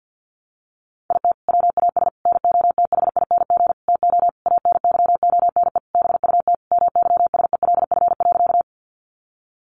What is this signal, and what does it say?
Morse code, heard as a keyed tone, about 50 wpm, 700 Hz. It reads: IT WAS NOTHING TO ANYONE BUT MYSELF.
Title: Sentences from Top 1300 Words 50wpm